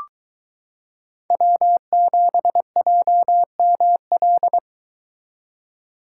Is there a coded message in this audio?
Yes: Morse code from 1.3 to 4.6 s, heard as a keyed tone, about 22 wpm, 700 Hz. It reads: W7JML